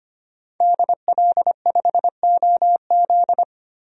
Morse code, heard as a keyed tone, about 25 wpm, 700 Hz. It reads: DL5OZ
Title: DXCC-100 Call Signs - 5 Characters - 25wpm